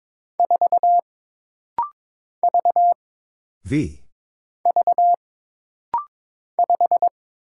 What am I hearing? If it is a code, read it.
Morse code, heard as a keyed tone, about 22 wpm, 700 Hz. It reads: 4 V V 5
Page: This rapid-fire timing is geared to push your ICR (Instant Character Recognition) skills.